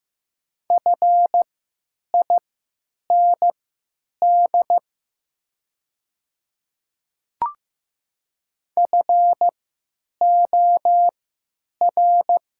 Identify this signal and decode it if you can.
Morse code, heard as a keyed tone, about 15 wpm, spaced as for 9 wpm, 690 Hz. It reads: FIND FOR
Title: 3x Character Spacing - Top 100 Words - 15wpm